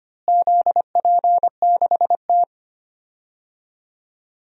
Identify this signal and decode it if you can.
Morse code, heard as a keyed tone, about 25 wpm, 700 Hz. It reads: ZP6T
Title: DXCC-100 Call Signs - 4 Characters - Repeated 2x - 25wpm